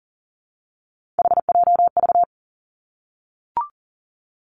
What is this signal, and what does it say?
Morse code, heard as a keyed tone, about 38 wpm, 700 Hz. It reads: HJV
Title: Three Challenging Letter-Number Combinations 40wpm